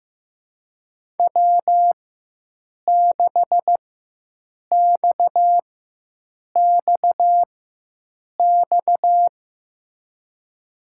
Morse code, heard as a keyed tone, about 15 wpm, spaced as for 7 wpm, 700 Hz. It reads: W6XXX
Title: USA Call Signs - 5 Characters - 4x Character Spacing - 15wpm